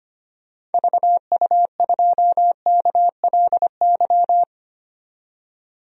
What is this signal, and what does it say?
Morse code, heard as a keyed tone, about 25 wpm, 700 Hz. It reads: VU2KLY
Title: DXCC-100 Call Signs - 6 Characters - 25wpm